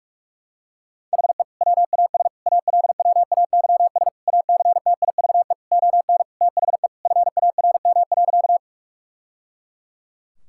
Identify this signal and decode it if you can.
Morse code, heard as a keyed tone, about 45 wpm, 700 Hz. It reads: HE WAS ALWAYS ACTIVE ON THE FARM.
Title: Sentences from Top 1000 Words - Review Part 6 - 45wpm